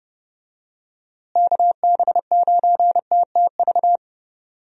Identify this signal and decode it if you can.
Morse code, heard as a keyed tone, about 30 wpm, 700 Hz. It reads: KB9TTV
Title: USA Call Signs - 6 Characters - Repeated 3x - 30wpm